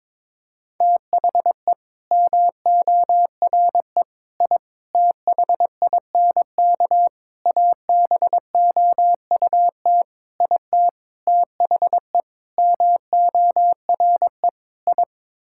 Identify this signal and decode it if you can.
Morse code, heard as a keyed tone, about 22 wpm, 700 Hz. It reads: THE MORE I THINK ABOUT IT THE MORE I
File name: Sets of 9 Words - 22wpm